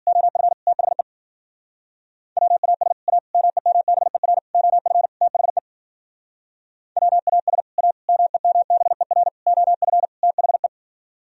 Morse code, heard as a keyed tone, about 50 wpm, 700 Hz. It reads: OF THE WAS A MEMBER OF THE WAS A MEMBER OF THE